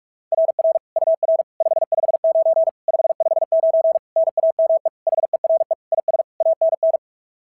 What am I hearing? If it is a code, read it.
Morse code, heard as a keyed tone, about 46 wpm, 690 Hz. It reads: RR UR 559 559 NAME HERE IS ANN
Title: SOTA QSO Practice 45wpm